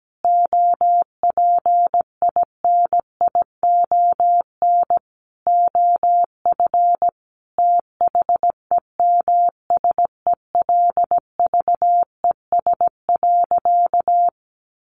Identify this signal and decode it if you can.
Morse code, heard as a keyed tone, about 17 wpm, 700 Hz. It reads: OPINION OF THEMSELVES.
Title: Sentences from Top 1300 Words 17wpm